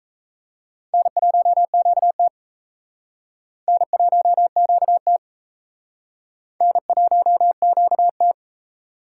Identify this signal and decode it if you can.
Morse code, heard as a keyed tone, about 35 wpm, 700 Hz. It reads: N1QT N1QT N1QT